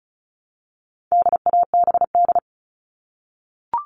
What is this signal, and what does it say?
Morse code, heard as a keyed tone, about 34 wpm, 700 Hz. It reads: DABD
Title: All Cut Numbers - 4-Digit 35wpm